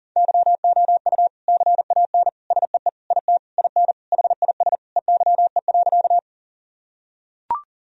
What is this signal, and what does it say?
Morse code, heard as a keyed tone, about 40 wpm, 700 Hz. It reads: YOU CAN SEE IT IN HIS EYE.
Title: Sentences from Top 1000 Words - Review Part 1 - 40wpm